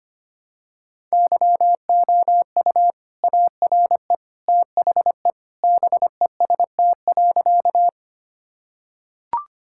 Morse code, heard as a keyed tone, about 25 wpm, 700 Hz. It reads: YOU ARE THE BEST.
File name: practice / 25wpm / Sentences with 2-Word Phrases - 25wpm